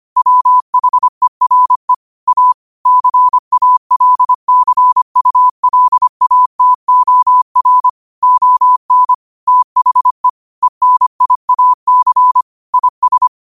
Morse code, heard as a keyed tone, about 25 wpm, 1000 Hz. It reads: WHERE A CALCULATOR ON THE ENIAC IS
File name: qotdfiles / 25wpm